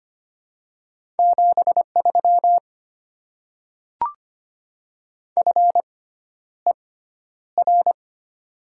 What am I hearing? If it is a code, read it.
Morse code, heard as a keyed tone, about 25 wpm, spaced as for 11 wpm, 700 Hz. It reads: 73 FER
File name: Top 20 QSO Elements - 6x Character Spacing - 25wpm